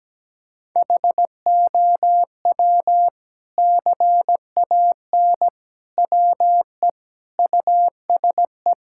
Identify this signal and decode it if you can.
Morse code, heard as a keyed tone, about 17 wpm, 700 Hz. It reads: HOW CAN WE USE
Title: Sentences from Top 1000 Words - Review Part 5 - 17wpm